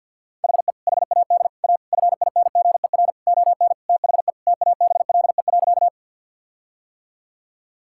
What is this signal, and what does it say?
Morse code, heard as a keyed tone, about 50 wpm, 700 Hz. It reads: HE HAD A FINGER ON THE TABLE.
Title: Sentences from Top 1600 Words 50wpm